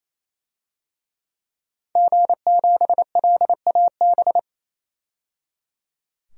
Morse code, heard as a keyed tone, about 28 wpm, 700 Hz. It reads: G7LAB
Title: DXCC-100 Call Signs - 5 Characters - 28wpm